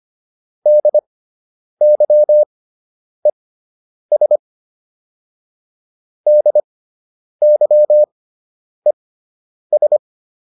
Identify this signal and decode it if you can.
Morse code, heard as a keyed tone, about 25 wpm, spaced as for 9 wpm, 600 Hz. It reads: DYES DYES